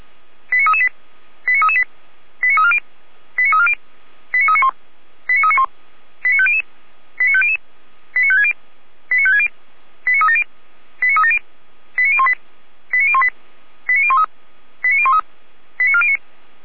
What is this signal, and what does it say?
Five-tone selective call (ZVEI), heard as quick runs of five tones, tones of about 70 ms. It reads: 89338 89338 89340 89340 89391 89391 89500 89500 89560 89560 89370 89370 89018 89018 89013 89013 89409
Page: SDR/ZVEI